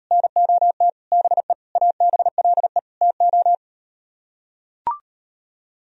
Morse code, heard as a keyed tone, about 39 wpm, 700 Hz. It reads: NOT BE ABLE TO